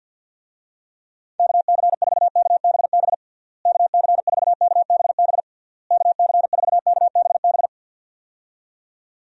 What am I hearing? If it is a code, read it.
Morse code, heard as a keyed tone, about 50 wpm, 700 Hz. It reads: KC4KBB KC4KBB KC4KBB
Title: USA Call Signs - 6 Characters - Repeated 3x - 50wpm